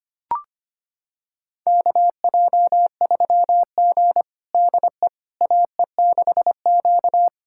Morse code, heard as a keyed tone, about 25 wpm, 700 Hz. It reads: KJ3G DE AE6Q
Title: USA Callsign DE Callsign - 4 Characters - 25wpm